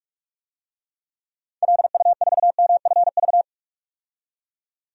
Morse code, heard as a keyed tone, about 43 wpm, 700 Hz. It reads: LU4MFV
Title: DXCC-100 Call Signs - 6 Characters - 45wpm